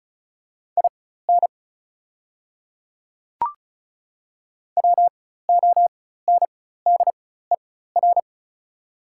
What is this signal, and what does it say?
Morse code, heard as a keyed tone, about 34 wpm, spaced as for 16 wpm, 700 Hz. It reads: IN WONDER
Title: 4x Character Spacing - Top 500 Words - 35wpm